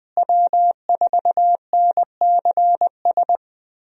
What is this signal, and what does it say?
Morse code, heard as a keyed tone, about 20 wpm, 700 Hz. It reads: W4NCS
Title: USA Call Signs - 5 Characters - Repeated 3x - 20wpm